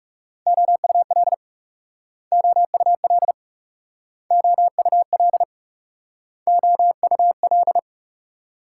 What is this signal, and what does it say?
Morse code, heard as a keyed tone, about 38 wpm, 700 Hz. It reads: OUL OUL OUL OUL